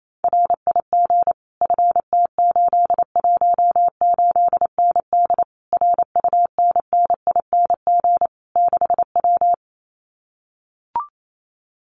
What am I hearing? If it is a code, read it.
Morse code, heard as a keyed tone, about 28 wpm, 700 Hz. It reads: RIG FT818ND RUNNING 6W
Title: Common QSO Phrases Encore 28wpm